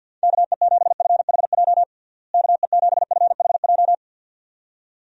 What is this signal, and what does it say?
Morse code, heard as a keyed tone, about 53 wpm, 700 Hz. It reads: KE7FHJ KE7FHJ
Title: USA Call Signs - 6 Characters - Repeated 2x - 50wpm